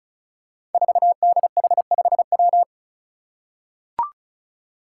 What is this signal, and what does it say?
Morse code, heard as a keyed tone, about 36 wpm, 700 Hz. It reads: 4DH5W
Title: Five Challenging Letter-Number Combinations 35wpm